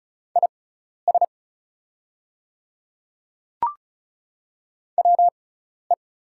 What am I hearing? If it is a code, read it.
Morse code, heard as a keyed tone, about 36 wpm, spaced as for 12 wpm, 700 Hz. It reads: IS WE
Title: 6x Character Spacing - Top 100 Words - 35wpm